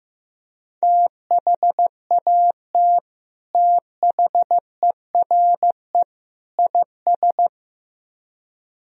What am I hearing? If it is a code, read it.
Morse code, heard as a keyed tone, about 16 wpm, 700 Hz. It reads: THAT THERE IS
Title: Speed-Racing - Sets of 3 Words 15wpm